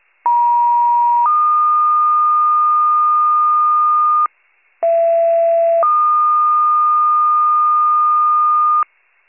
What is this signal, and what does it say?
Two-tone page, heard as a one-second tone, then a three-second tone, about 950 then 1220 Hz, about 670 then 1220 Hz. It is heard from 0.3 to 8.8 s.